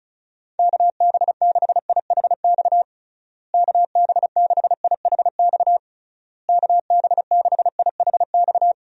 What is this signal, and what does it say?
Morse code, heard as a keyed tone, about 36 wpm, 700 Hz. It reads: KB6IHX KB6IHX KB6IHX
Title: USA Call Signs - 6 Characters - Repeated 3x - 35wpm